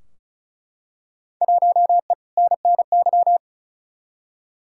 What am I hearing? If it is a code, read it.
Morse code, heard as a keyed tone, about 34 wpm, 700 Hz. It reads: 1E NNY
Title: Field Day Exchanges - 35wpm